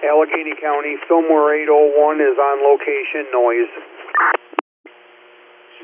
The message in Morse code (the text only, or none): none